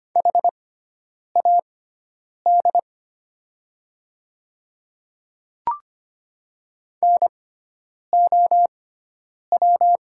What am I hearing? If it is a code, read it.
Morse code, heard as a keyed tone, about 25 wpm, spaced as for 9 wpm, 700 Hz. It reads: HAD NOW
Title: 6x Character Spacing - Top 100 Words - 25wpm